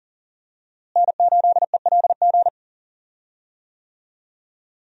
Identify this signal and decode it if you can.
Morse code, heard as a keyed tone, about 40 wpm, 700 Hz. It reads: N8ELG
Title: USA Call Signs - 5 Characters - Repeated 3x - 40wpm